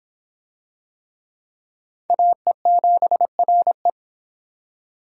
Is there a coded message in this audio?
Yes: Morse code from 2.1 to 3.9 s, heard as a keyed tone, about 24 wpm, 700 Hz. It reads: AE7RE